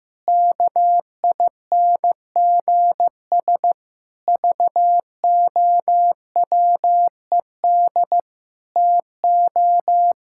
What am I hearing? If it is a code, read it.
Morse code, heard as a keyed tone, about 15 wpm, 700 Hz. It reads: KINGS VOWED TO
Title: Pangram Challenge v2 - 15wpm